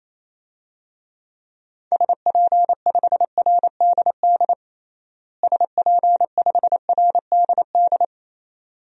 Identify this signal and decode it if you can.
Morse code, heard as a keyed tone, about 28 wpm, 700 Hz. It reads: SP5RDD SP5RDD
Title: DXCC-100 Call Signs - 6 Characters - Repeated 2x - 28wpm